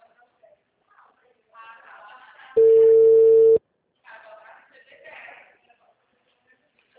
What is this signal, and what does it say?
Ringback tone (China, 450 Hz), with speech.